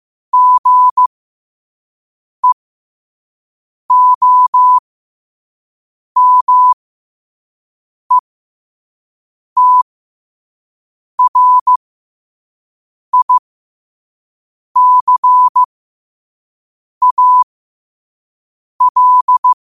Morse code, heard as a keyed tone, about 15 wpm, spaced as for 5 wpm, 1000 Hz. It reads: GEOMETRICAL